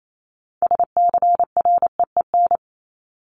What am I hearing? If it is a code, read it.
Morse code, heard as a keyed tone, about 28 wpm, 700 Hz. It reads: SCREEN
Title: Top 751-1000 Words 28wpm